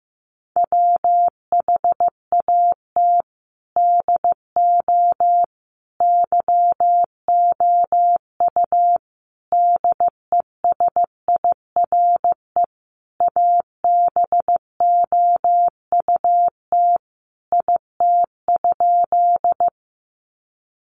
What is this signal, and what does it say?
Morse code, heard as a keyed tone, about 15 wpm, 700 Hz. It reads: WHAT DO YOU DESIRE ABOUT IT?
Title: Sentences from Top 1100 Words 15wpm